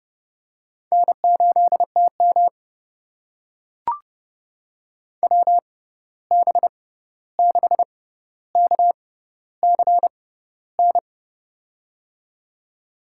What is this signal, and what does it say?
Morse code, heard as a keyed tone, about 32 wpm, spaced as for 14 wpm, 700 Hz. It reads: N8TM WB6KCN
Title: USA Call Signs - 6x Character Spacing - 30wpm